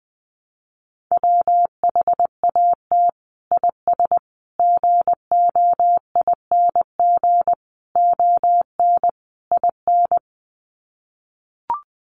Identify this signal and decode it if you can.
Morse code, heard as a keyed tone, about 20 wpm, 700 Hz. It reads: WHAT IS GOING ON IN
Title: Sets of 5 Words Encore2 20wpm